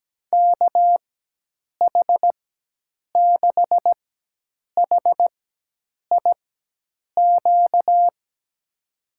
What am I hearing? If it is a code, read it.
Morse code, heard as a keyed tone, about 17 wpm, spaced as for 8 wpm, 700 Hz. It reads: KH6HIQ
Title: USA Call Signs - 4x Character Spacing - 17wpm